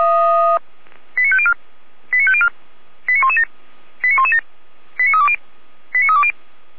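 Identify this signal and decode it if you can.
Five-tone selective call (ZVEI), heard as quick runs of five tones, tones of about 70 ms. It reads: 89504 89504 89117 89117 89320 89320
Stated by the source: ZVEI five-tone selective call received on an SDR.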